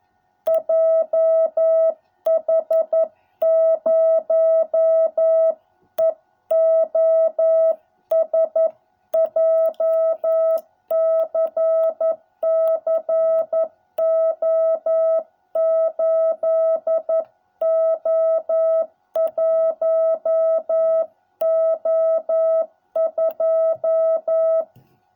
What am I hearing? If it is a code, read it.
Morse code, heard as a keyed tone, about 11 wpm, 660 Hz. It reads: JH0EOS JCC O8O1O2